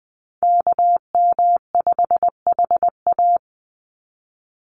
Morse code, heard as a keyed tone, about 20 wpm, 700 Hz. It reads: KM5HA
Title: USA Call Signs 20wpm